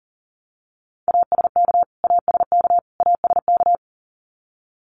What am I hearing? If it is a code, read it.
Morse code, heard as a keyed tone, about 40 wpm, 700 Hz. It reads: ASK ASK ASK
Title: Top 500 Words Repeated 40wpm